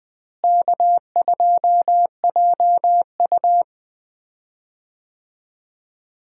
Morse code, heard as a keyed tone, about 20 wpm, 700 Hz. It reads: K2JU
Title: USA Call Signs - 4 Characters - Repeated 2x - 20wpm